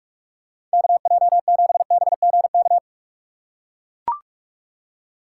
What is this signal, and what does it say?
Morse code, heard as a keyed tone, about 44 wpm, 700 Hz. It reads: KJ7BGK
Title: USA Call Signs - 4x Character Spacing - 45wpm